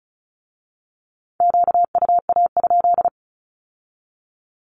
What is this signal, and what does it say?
Morse code, heard as a keyed tone, about 36 wpm, 700 Hz. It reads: QUA?
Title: Q-Codes with Spoken Meaning - 35wpm